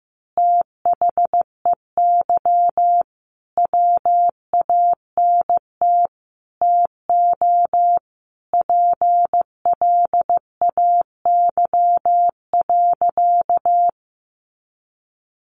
Morse code, heard as a keyed tone, about 15 wpm, 700 Hz. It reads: THEY WANT TO PLAY.
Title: Easy Phrases 15wpm